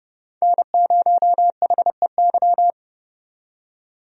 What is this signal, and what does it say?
Morse code, heard as a keyed tone, about 30 wpm, 700 Hz. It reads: N0HEY
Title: USA Call Signs - 5 Characters - 30wpm